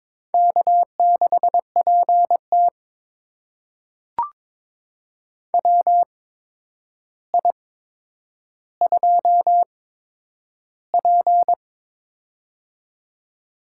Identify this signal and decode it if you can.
Morse code, heard as a keyed tone, about 22 wpm, spaced as for 9 wpm, 700 Hz. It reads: K6PT WI2P
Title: USA Call Signs - 4 Characters - 8x Character Spacing - 22wpm